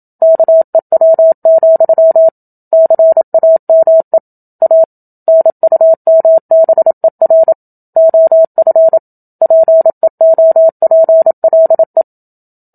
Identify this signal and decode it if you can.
Morse code, heard as a keyed tone, about 27 wpm, 650 Hz. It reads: KEW, CAME A NUMBER OF PEOPLE
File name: Book Two / 01-Chapter 1 - 27 WPM